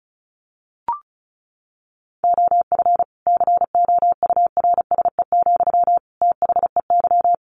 Morse code, heard as a keyed tone, about 34 wpm, 700 Hz. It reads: OF COURSE, THEY
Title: Sentences from Top 1400 Words 35wpm